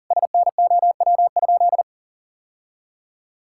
Morse code, heard as a keyed tone, about 40 wpm, 700 Hz. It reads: SNOW?
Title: Sentences from Top 1600 Words 40wpm